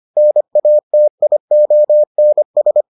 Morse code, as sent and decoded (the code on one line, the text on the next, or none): -. .- - .. --- -. ...
NATIONS